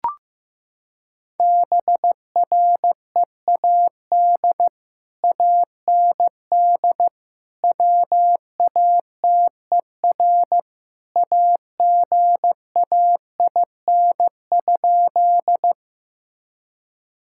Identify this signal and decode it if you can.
Morse code, heard as a keyed tone, about 15 wpm, 700 Hz. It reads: BREAD AND WATER AGAIN?
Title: Sentences from Top 1600 Words 15wpm